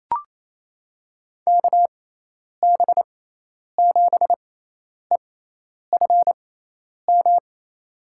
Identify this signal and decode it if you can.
Morse code, heard as a keyed tone, about 28 wpm, spaced as for 10 wpm, 700 Hz. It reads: KB7EFM
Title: USA Call Signs - 6x Character Spacing - 28wpm